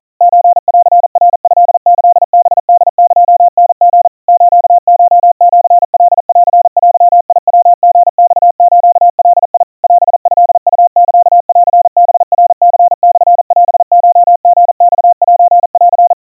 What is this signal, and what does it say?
Morse code (Japanese Wabun code), heard as a keyed tone, about 40 wpm, 700 Hz. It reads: ソンナトキハホエタリ スコシカンデヤリマスガ オトウサンハナニモオコリマセン
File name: kawa / Morse sample / kamyu 150 cpm 700 Hz Crispy